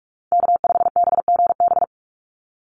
Morse code, heard as a keyed tone, about 44 wpm, 700 Hz. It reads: K5BGB
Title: Call Signs - 45wpm